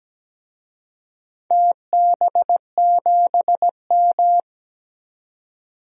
Morse code, heard as a keyed tone, about 17 wpm, 700 Hz. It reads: TB7M
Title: DXCC-100 Call Signs - 4 Characters - 17wpm